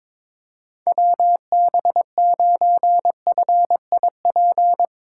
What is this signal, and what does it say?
Morse code, heard as a keyed tone, about 22 wpm, 700 Hz. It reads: WB9FIP
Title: K1USN SST QSO Practice - 22wpm